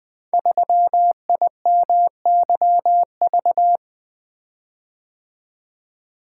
Morse code, heard as a keyed tone, about 20 wpm, 700 Hz. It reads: 3IMYV
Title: Code Groups - 5 Letter:Numbers - 20wpm